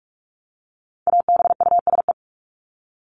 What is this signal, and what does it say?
Morse code, heard as a keyed tone, about 44 wpm, 700 Hz. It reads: ABUSE
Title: Top 1251-1500 Words 45wpm